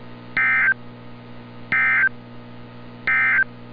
Emergency Alert System (SAME) tones, heard as harsh buzzing bursts. It is heard from 0.3 to 3.5 s.
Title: NOAA Weather Radio - Altoona, PA